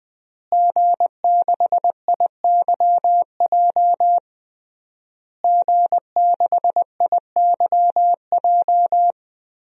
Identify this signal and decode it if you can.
Morse code, heard as a keyed tone, about 20 wpm, 700 Hz. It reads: G6IYJ G6IYJ
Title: DXCC-100 Call Signs - 5 Characters - Repeated 3x - 20wpm